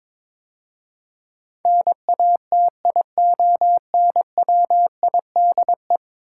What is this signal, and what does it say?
Morse code, heard as a keyed tone, about 22 wpm, 700 Hz. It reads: NATIONWIDE
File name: practice / 22wpm / 10 Letter Words - First 3 Letters Given - 22wpm